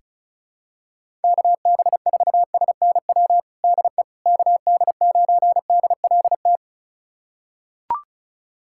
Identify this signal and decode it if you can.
Morse code, heard as a keyed tone, about 36 wpm, 700 Hz. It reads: KB4SNW DE KD9DLT
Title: USA Callsign DE Callsign 35wpm